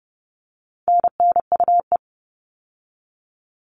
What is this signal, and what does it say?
Morse code, heard as a keyed tone, about 30 wpm, 700 Hz. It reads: NNUE
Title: All Cut Numbers - 4-Digit 30wpm